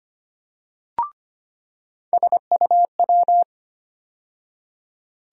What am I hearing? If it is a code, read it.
Morse code, heard as a keyed tone, about 25 wpm, 700 Hz. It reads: SUW
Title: USA Call Sign Postfixes - 3 Letters 25wpm